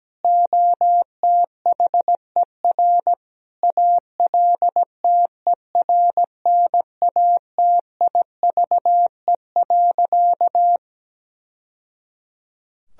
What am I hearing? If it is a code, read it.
Morse code, heard as a keyed tone, about 17 wpm, 700 Hz. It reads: OTHER ALTERNATIVE.
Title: Sentences from Top 1000 Words - Review Part 4 - 17wpm